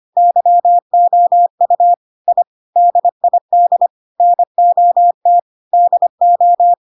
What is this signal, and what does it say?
Morse code, heard as a keyed tone, about 25 wpm, 700 Hz. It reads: YOU I DID NOT DO
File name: Pinocchio / Pinocchio ch2 25 25 ik1hge